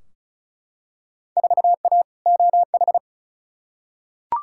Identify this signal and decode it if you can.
Morse code, heard as a keyed tone, about 36 wpm, 700 Hz. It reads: 4A OH
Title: Field Day Exchanges - 35wpm